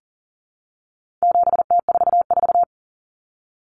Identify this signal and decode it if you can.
Morse code, heard as a keyed tone, about 40 wpm, 700 Hz. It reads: 7T44
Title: Common Cut Numbers - 4-Digit 40wpm